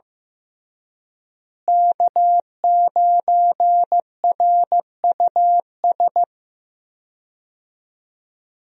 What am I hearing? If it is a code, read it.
Morse code, heard as a keyed tone, about 15 wpm, 700 Hz. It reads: K9RUS
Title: USA Call Signs - 5 Characters - Repeated 3x - 15wpm